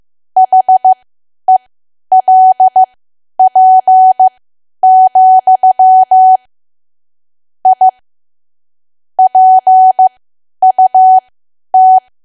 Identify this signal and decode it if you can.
Morse code, heard as a keyed tone, about 15 wpm, spaced as for 10 wpm, 750 Hz. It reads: HELP, I PUT
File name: cwa / W1AW-Level2 / 10WPM